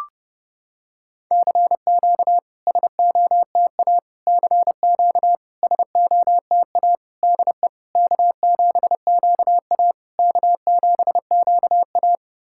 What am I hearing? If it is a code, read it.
Morse code, heard as a keyed tone, about 30 wpm, 690 Hz. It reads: CQ SOTA CQ SOTA DE K7QA K7QA
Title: SOTA QSO Practice 30wpm